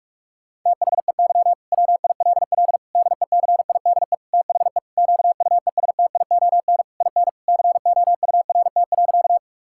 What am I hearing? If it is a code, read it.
Morse code, heard as a keyed tone, about 41 wpm, 700 Hz. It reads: THEY WILL DECIDE THE QUESTION IN COURT.
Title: Sentences from Top 1600 Words 45wpm